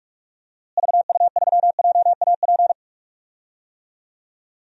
Morse code, heard as a keyed tone, about 41 wpm, 700 Hz. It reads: VU3JAP
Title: DXCC-100 Call Signs - 6 Characters - Repeated 3x - 45wpm